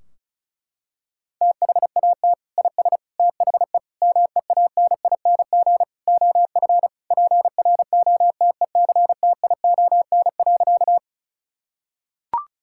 Morse code, heard as a keyed tone, about 36 wpm, 700 Hz. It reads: THAT IS THE MEANING OF PROTECTION.
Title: Sentences from Top 1600 Words 35wpm